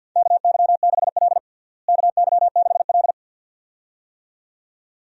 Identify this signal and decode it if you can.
Morse code, heard as a keyed tone, about 46 wpm, 700 Hz. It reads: KY6L KY6L
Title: USA Call Signs - 4 Characters - Repeated 2x - 50wpm